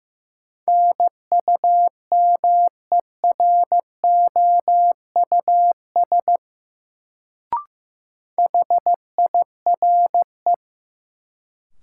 Morse code, heard as a keyed tone, about 15 wpm, 700 Hz. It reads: NUMEROUS HIRE
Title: Sentences from Top 1400 Words - New Word Drill Alt - 15wpm